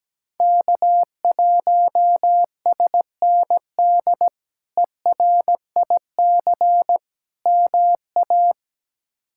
Morse code, heard as a keyed tone, about 17 wpm, 700 Hz. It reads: K1SND ERIC MA
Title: K1USN SST QSO Practice - 17wpm